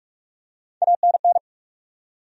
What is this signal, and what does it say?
Morse code, heard as a keyed tone, about 46 wpm, 700 Hz. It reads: ANN
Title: USA Common Names - Max 3 Letters - 45wpm